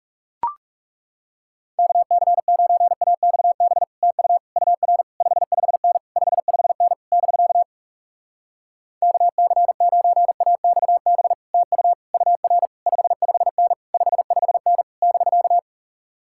Morse code, heard as a keyed tone, about 38 wpm, 700 Hz. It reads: KC9AXB TU UR 55N 55N <BK> KC9AXB TU UR 55N 55N <BK>